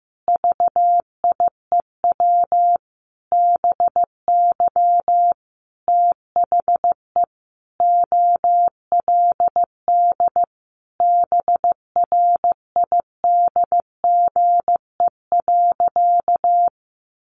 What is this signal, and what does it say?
Morse code, heard as a keyed tone, about 15 wpm, 700 Hz. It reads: VIEW BY THE OLD BRIDGE.
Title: Sentences from Top 1200 Words 15wpm